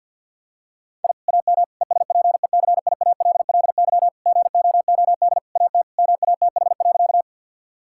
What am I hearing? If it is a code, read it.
Morse code, heard as a keyed tone, about 50 wpm, 700 Hz. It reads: I AM ESPECIALLY GOOD AT MATH.
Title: Sentences from Top 1000 Words - Review Part 6 - 50wpm